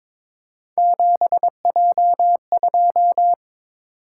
Morse code, heard as a keyed tone, about 22 wpm, 700 Hz. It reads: 7J2
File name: practice / 22wpm / Three Letter-Number Combinations - 22wpm